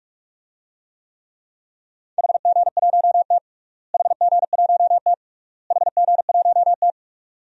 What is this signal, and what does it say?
Morse code, heard as a keyed tone, about 45 wpm, 700 Hz. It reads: HG1T HG1T HG1T
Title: DXCC-100 Call Signs - 4 Characters - Repeated 3x - 45wpm